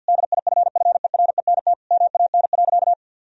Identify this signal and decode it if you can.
Morse code, heard as a keyed tone, about 50 wpm, 700 Hz. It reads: DIFFERENT MAN.
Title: Sentences from Top 1000 Words - Review Part 3 - 50wpm